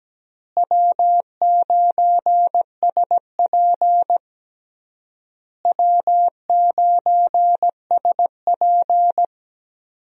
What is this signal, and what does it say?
Morse code, heard as a keyed tone, about 17 wpm, 700 Hz. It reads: W9SP W9SP
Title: USA Call Signs - 4 Characters - Repeated 3x - 17wpm